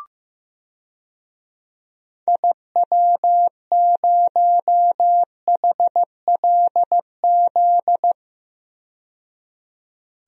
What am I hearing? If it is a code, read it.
Morse code, heard as a keyed tone, about 15 wpm, 690 Hz. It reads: IW0HLZ
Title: DXCC-100 Call Signs - 6 Characters - 15wpm